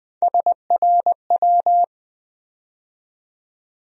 Morse code, heard as a keyed tone, about 20 wpm, 700 Hz. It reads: SRW
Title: USA Call Sign Postfixes - 3 Letters 20wpm